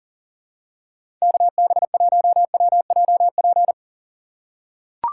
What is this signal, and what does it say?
Morse code, heard as a keyed tone, about 40 wpm, 700 Hz. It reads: KB1WJP
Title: USA Call Signs - 2x Character Spacing - 40wpm